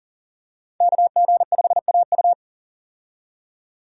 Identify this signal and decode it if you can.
Morse code, heard as a keyed tone, about 40 wpm, 700 Hz. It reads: KG5AU
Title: USA Call Signs - 5 Characters - 40wpm